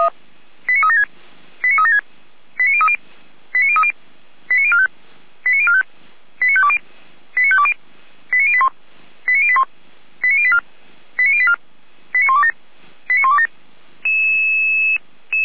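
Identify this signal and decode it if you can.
Five-tone selective call (ZVEI), heard as quick runs of five tones, tones of about 70 ms. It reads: 89376 89376 89030 89030 89045 89045 89520 89520 89081 89081 89084 89084 89127 89127
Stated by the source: ZVEI five-tone call received by SDR.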